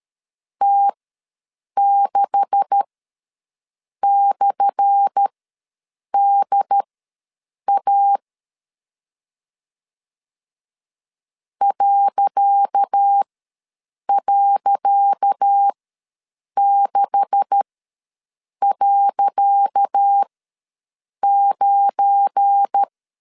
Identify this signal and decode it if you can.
Morse code, heard as a keyed tone, about 13 wpm, spaced as for 7 wpm, 800 Hz. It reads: T6/DA ..6.9
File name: LEZIONI / 10 SIGNS / 10C DOT ws 12wpm 3548